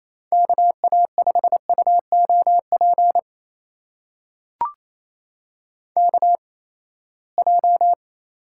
Morse code, heard as a keyed tone, about 28 wpm, spaced as for 19 wpm, 700 Hz. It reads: KA5UOP KJ
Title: USA Call Signs - 6 Characters - 8x Character Spacing - 28wpm